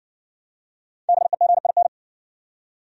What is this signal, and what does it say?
Morse code, heard as a keyed tone, about 60 wpm, 700 Hz. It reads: BEGIN